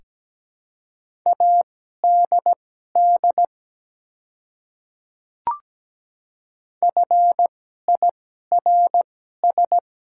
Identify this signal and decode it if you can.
Morse code, heard as a keyed tone, about 17 wpm, spaced as for 12 wpm, 700 Hz. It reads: ADD FIRS
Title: 2x Character Spacing - Top 200 Words - 17wpm